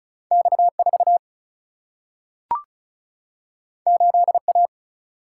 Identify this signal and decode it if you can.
Morse code, heard as a keyed tone, about 36 wpm, 700 Hz. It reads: X4 8A
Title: Single-Letter Single-Number 35wpm